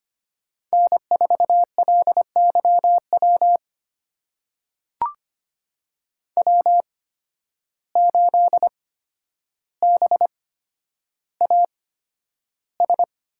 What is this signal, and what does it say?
Morse code, heard as a keyed tone, about 25 wpm, spaced as for 11 wpm, 700 Hz. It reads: N4LYW W8BAS
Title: USA Call Signs - 5 Characters - 8x Character Spacing - 25wpm